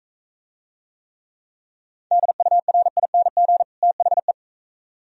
Morse code, heard as a keyed tone, about 42 wpm, 700 Hz. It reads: DURING THE